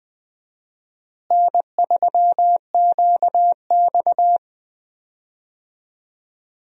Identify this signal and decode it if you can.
Morse code, heard as a keyed tone, about 21 wpm, 700 Hz. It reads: N3QX
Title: USA Call Signs - 4 Characters - Repeated 3x - 20wpm